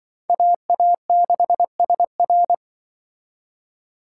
Morse code, heard as a keyed tone, about 25 wpm, 700 Hz. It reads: AA6SR
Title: Speed Racing - USA 2x2 Call Signs 20wpm